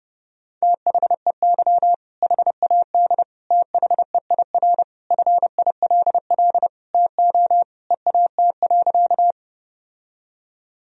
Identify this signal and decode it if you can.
Morse code, heard as a keyed tone, about 30 wpm, 700 Hz. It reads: THEY HAD THEIR FILL TO EAT.
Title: Sentences from Top 1000 Words - Review Part 3 - 30wpm